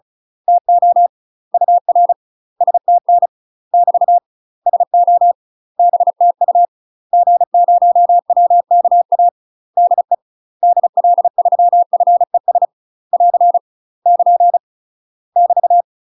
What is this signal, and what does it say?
Morse code, heard as a keyed tone, about 35 wpm, 700 Hz. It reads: TO UR STN <BT> SO BTU G0WKA DE DL3FES <AR> <KN> <BT>
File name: UK MASTER QSOs TRAINING / 35WPM / UK MASTER QSOs 0015